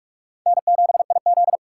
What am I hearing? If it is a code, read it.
Morse code, heard as a keyed tone, about 42 wpm, 700 Hz. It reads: N7IZ